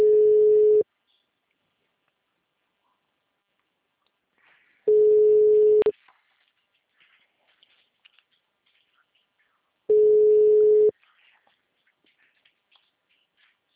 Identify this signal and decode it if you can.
Ringback tone (Europe, 425 Hz), with speech.